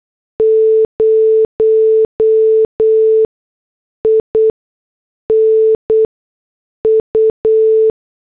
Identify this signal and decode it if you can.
Morse code, heard as a keyed tone, about 8 wpm, 440 Hz. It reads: 0 I N U
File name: morse / headlines